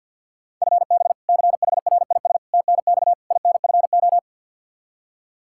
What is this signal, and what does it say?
Morse code, heard as a keyed tone, about 52 wpm, 700 Hz. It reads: FB CHRIS TNX INFO
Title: Practice QSO - Exchange 3 - More Info - 50wpm